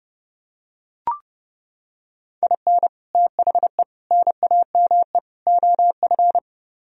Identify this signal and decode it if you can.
Morse code, heard as a keyed tone, about 29 wpm, 700 Hz. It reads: IN THE NAME OF